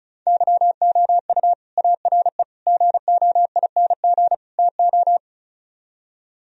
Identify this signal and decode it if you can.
Morse code, heard as a keyed tone, about 42 wpm, 700 Hz. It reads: YOU ARE GOING TO
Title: Speed-Racing - Sets of 4 Words 35wpm